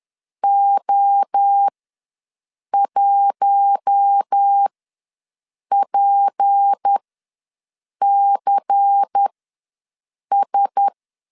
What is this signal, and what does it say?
Morse code, heard as a keyed tone, about 10 wpm, spaced as for 6 wpm, 800 Hz. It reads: O1PCS
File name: LEZIONI / 6 PKX0 / 6 P ws 10wpm 3751